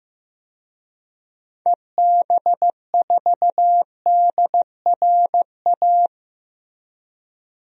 Morse code, heard as a keyed tone, about 15 wpm, 700 Hz. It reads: EB4DRA
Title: DXCC-100 Call Signs - 6 Characters - Repeated 3x - 15wpm